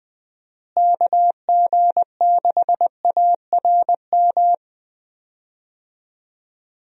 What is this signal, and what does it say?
Morse code, heard as a keyed tone, about 20 wpm, 700 Hz. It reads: KG6ARM
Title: USA Call Signs - 6 Characters - Repeated 2x - 20wpm